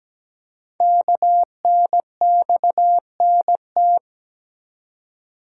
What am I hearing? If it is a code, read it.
Morse code, heard as a keyed tone, about 17 wpm, 700 Hz. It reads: KNXNT
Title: Code Groups - 5 Letters - 17wpm